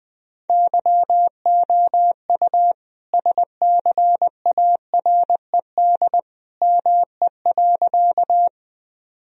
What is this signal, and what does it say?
Morse code, heard as a keyed tone, about 21 wpm, 700 Hz. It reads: YOU SCARED ME.
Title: Sentences from Top 1000 Words - Review Part 4 - 20wpm